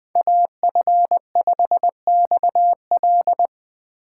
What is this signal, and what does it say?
Morse code, heard as a keyed tone, about 24 wpm, 700 Hz. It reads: AF5XL
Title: Speed Racing - USA 2x2 Call Signs 20wpm